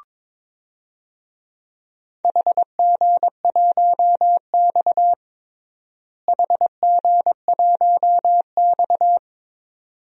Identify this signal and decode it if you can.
Morse code, heard as a keyed tone, about 22 wpm, 700 Hz. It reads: HG1X HG1X
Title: DXCC-100 Call Signs - 4 Characters - Repeated 2x - 22wpm